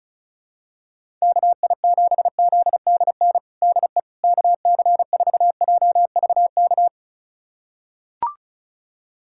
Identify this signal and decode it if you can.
Morse code, heard as a keyed tone, about 36 wpm, 700 Hz. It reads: KI7ZDN DE KC4JVK
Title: USA Callsign DE Callsign - 6 Characters - 35wpm